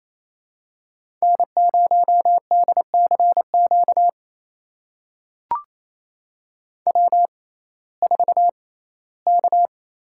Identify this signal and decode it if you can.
Morse code, heard as a keyed tone, about 28 wpm, spaced as for 17 wpm, 700 Hz. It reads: N0DCQ W4K
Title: USA Call Signs - 5 Characters - 6x Character Spacing - 28wpm